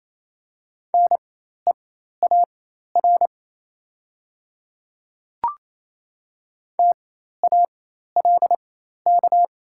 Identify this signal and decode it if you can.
Morse code, heard as a keyed tone, about 28 wpm, spaced as for 13 wpm, 700 Hz. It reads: NEAR TALK
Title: 4x Character Spacing - Top 500 Words - 28wpm